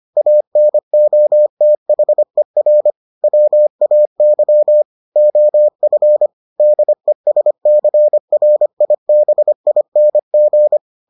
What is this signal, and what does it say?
Morse code, heard as a keyed tone, about 25 wpm, 600 Hz. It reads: ANOTHER WAY OF DESCRIBING